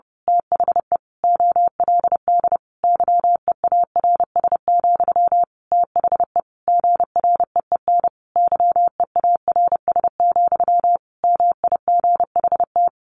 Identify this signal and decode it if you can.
Morse code, heard as a keyed tone, about 30 wpm, 700 Hz. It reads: THE OLD YEARS, THE GREEN YEARS, MIGHT
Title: A Sound Of Thunder 30wpm